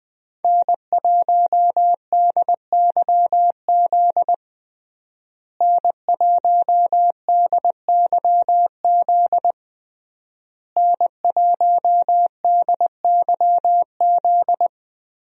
Morse code, heard as a keyed tone, about 21 wpm, 700 Hz. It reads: N1DYZ N1DYZ N1DYZ
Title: USA Call Signs - 5 Characters - Repeated 3x - 20wpm